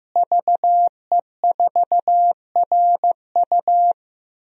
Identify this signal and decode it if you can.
Morse code, heard as a keyed tone, about 15 wpm, 700 Hz. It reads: VE4RU
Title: DXCC-100 Call Signs - 5 Characters - Repeated 3x - 15wpm